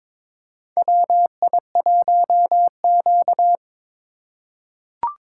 Morse code, heard as a keyed tone, about 22 wpm, 700 Hz. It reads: WI1Q
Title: USA Call Signs - 3x Character Spacing - 22wpm